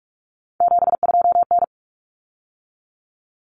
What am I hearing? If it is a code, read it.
Morse code, heard as a keyed tone, about 44 wpm, 700 Hz. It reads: 72N
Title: Common Cut Numbers - 3-Digit 45wpm